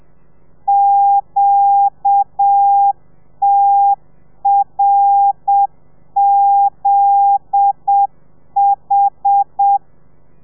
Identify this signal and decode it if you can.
Morse code, heard as a keyed tone, about 7 wpm, 800 Hz. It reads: QTRZH